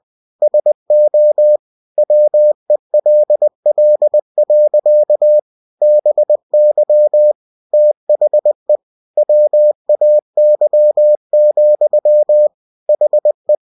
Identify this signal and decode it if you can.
Morse code, heard as a keyed tone, about 20 wpm, 600 Hz. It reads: SO WELL. BY THE WAY, HE
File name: cw-books / winnie-the-pooh / winnie-20wpm-page-0065